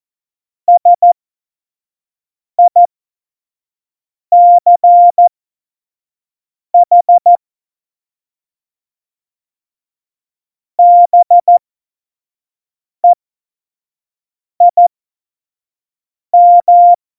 Morse code, heard as a keyed tone, about 14 wpm, spaced as for 5 wpm, 700 Hz. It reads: SICH BEIM
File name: Morsekurs / Witze / Witz 21 Juli 2020, 25 F70